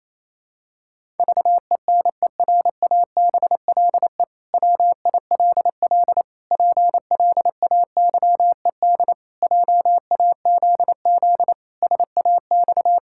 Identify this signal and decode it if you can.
Morse code, heard as a keyed tone, about 28 wpm, 700 Hz. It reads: VENERABLE WILL PLAYED JAZZ SAX
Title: Pangram Challenge v2 - 28wpm